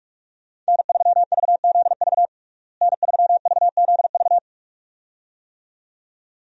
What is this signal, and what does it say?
Morse code, heard as a keyed tone, about 41 wpm, 700 Hz. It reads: N3VZV N3VZV
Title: USA Call Signs - 5 Characters - Repeated 2x - 45wpm